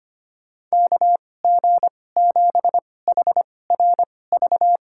Morse code, heard as a keyed tone, about 25 wpm, spaced as for 18 wpm, 690 Hz. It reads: KG7HRV
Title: USA Call Signs - 6 Characters - 2x Character Spacing - 25wpm